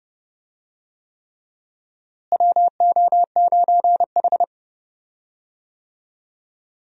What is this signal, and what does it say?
Morse code, heard as a keyed tone, about 30 wpm, 700 Hz. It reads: WO9H
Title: USA Call Signs - 4 Characters - Repeated 3x - 30wpm